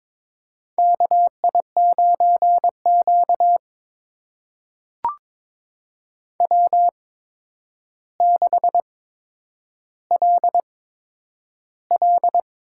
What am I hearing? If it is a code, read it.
Morse code, heard as a keyed tone, about 22 wpm, spaced as for 9 wpm, 700 Hz. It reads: KI9Q W6LL
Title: USA Call Signs - 4 Characters - 8x Character Spacing - 22wpm